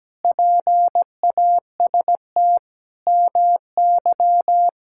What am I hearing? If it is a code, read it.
Morse code, heard as a keyed tone, about 17 wpm, 700 Hz. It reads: PAST MY
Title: Pangram Challenge v2 - 17wpm